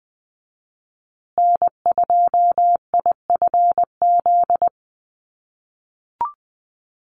Morse code, heard as a keyed tone, about 20 wpm, 700 Hz. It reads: N2IFZ
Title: USA Call Signs Repeated 3x 20wpm